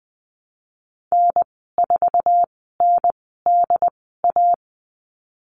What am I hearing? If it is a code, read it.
Morse code, heard as a keyed tone, about 20 wpm, spaced as for 15 wpm, 700 Hz. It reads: N4NDA
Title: 076 Lesson 13g - Letter L - Call Signs 20wpm